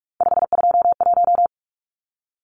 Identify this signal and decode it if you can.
Morse code, heard as a keyed tone, about 44 wpm, 700 Hz. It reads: 521
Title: Numbers - 3-Digit 45wpm